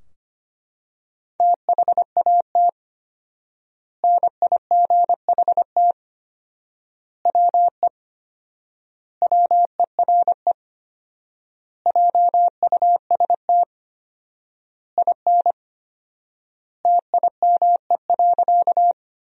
Morse code, heard as a keyed tone, about 25 wpm, 700 Hz. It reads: THAT NIGHT WE WERE JUST IN TIME.
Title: 4x Word-Spacing - Sentences from Top 300 Words - 25wpm